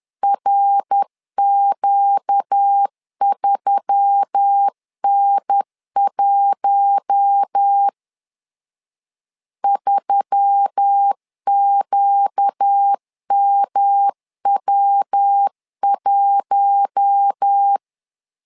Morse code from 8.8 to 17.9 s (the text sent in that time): 3QMW1